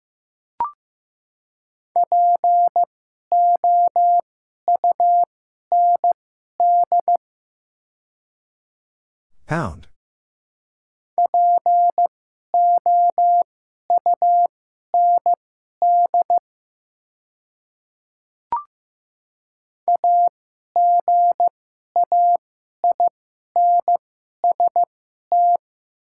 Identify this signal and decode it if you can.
Morse code, heard as a keyed tone, about 15 wpm, spaced as for 11 wpm, 700 Hz. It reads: POUND POUND AGAINST
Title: 2x Character Spacing - Top 500 Words - 15wpm